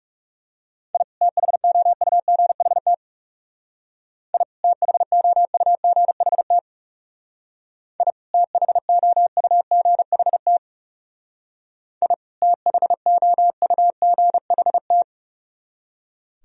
Morse code, heard as a keyed tone, about 39 wpm, 700 Hz. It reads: I THOUGHT I THOUGHT I THOUGHT I THOUGHT